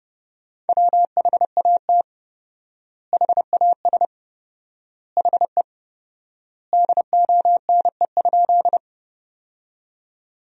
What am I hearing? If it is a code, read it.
Morse code, heard as a keyed tone, about 31 wpm, 700 Hz. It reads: WHAT HAS HE DONE?
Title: 4x Word-Spacing - Sentences from Top 500 Words - 30wpm